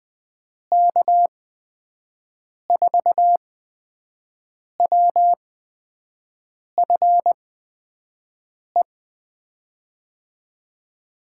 Morse code, heard as a keyed tone, about 20 wpm, spaced as for 5 wpm, 700 Hz. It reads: K4WFE
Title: USA Call Signs - 6 Characters - 8x Character Spacing - 20wpm